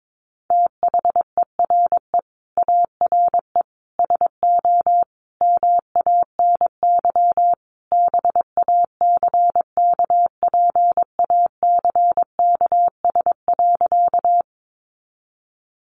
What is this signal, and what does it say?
Morse code, heard as a keyed tone, about 22 wpm, 700 Hz. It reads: THERE ARE SO MANY BACKPACKS.
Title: Additional Easy Phrases 22wpm